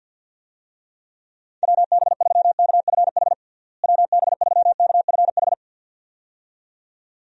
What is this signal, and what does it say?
Morse code, heard as a keyed tone, about 49 wpm, 700 Hz. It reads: WB3KFH WB3KFH
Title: USA Call Signs - 6 Characters - Repeated 2x - 50wpm